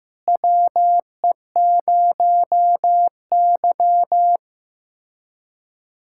Morse code, heard as a keyed tone, about 15 wpm, 700 Hz. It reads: WE0Y
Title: USA Callsign DE Callsign - 4 Characters - 15wpm